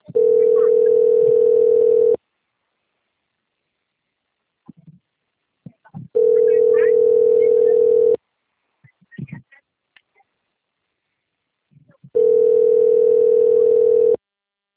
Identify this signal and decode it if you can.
Ringback tone (North America), 440 + 480 Hz.